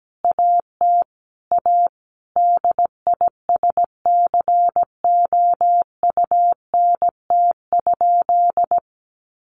Morse code, heard as a keyed tone, about 17 wpm, 700 Hz. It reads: AT A DISCOUNT?
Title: Sentences from Top 1200 Words 17wpm